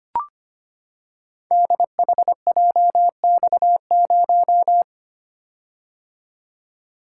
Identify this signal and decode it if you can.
Morse code, heard as a keyed tone, about 25 wpm, 700 Hz. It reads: DHJX0
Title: Code Groups - 5 Letter:Numbers - 25wpm